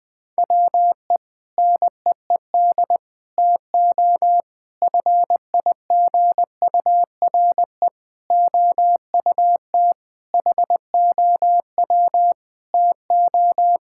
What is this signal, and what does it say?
Morse code, heard as a keyed tone, about 20 wpm, 700 Hz. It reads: WE NEED TO FIGURE OUT HOW TO
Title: Sentences with 5-Word Phrases - 20wpm